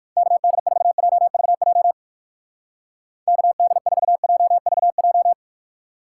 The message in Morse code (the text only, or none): KD4JVJ KD4JVJ